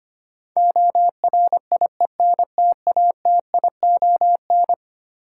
Morse code, heard as a keyed tone, about 25 wpm, 700 Hz. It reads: ORIENTATION
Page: morse-code-ninja-practice-25wpm